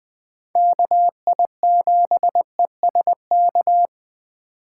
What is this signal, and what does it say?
Morse code, heard as a keyed tone, about 21 wpm, 700 Hz. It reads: KI7ESK
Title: Speed Racing - USA 2x3 Call Signs 17wpm